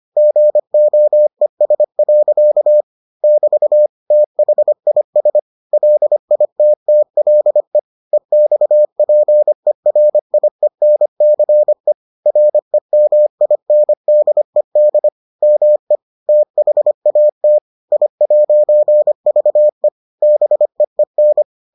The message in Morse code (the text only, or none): GOES. <BT> THIS LITTLE EXPERIENCE REMINDED ME THAT I'VE BEEN